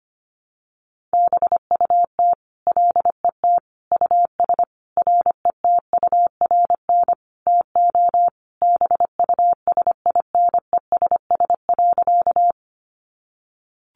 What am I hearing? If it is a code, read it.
Morse code, heard as a keyed tone, about 25 wpm, 700 Hz. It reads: BUT LET US RETURN TO BUSINESS.
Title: Sentences from Top 800 Words 25wpm